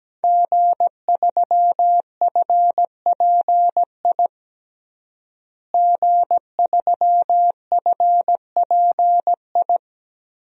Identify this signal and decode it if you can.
Morse code, heard as a keyed tone, about 17 wpm, 700 Hz. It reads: G3FPI G3FPI
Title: DXCC-100 Call Signs - 5 Characters - Repeated 2x - 17wpm